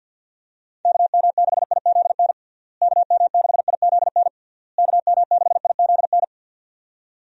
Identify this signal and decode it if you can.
Morse code, heard as a keyed tone, about 50 wpm, 700 Hz. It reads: KM6IZN KM6IZN KM6IZN
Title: USA Call Signs - 6 Characters - Repeated 3x - 50wpm